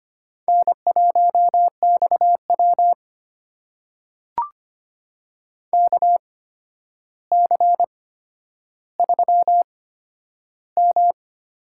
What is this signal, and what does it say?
Morse code, heard as a keyed tone, about 25 wpm, spaced as for 11 wpm, 700 Hz. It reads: N1XW KC3M
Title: USA Call Signs - 8x Character Spacing - 25wpm